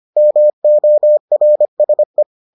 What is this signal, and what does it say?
Morse code, heard as a keyed tone, about 25 wpm, 600 Hz. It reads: MORSE